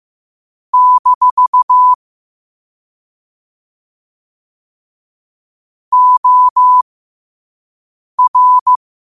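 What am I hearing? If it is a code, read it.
Morse code, heard as a keyed tone, about 15 wpm, spaced as for 5 wpm, 1000 Hz. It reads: - OR
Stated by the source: Morse quote of the day at 5 wpm.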